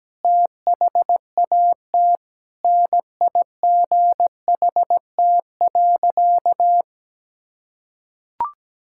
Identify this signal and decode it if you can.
Morse code, heard as a keyed tone, about 17 wpm, 700 Hz. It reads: THAT NIGHT.
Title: Sentences from Top 1600 Words 17wpm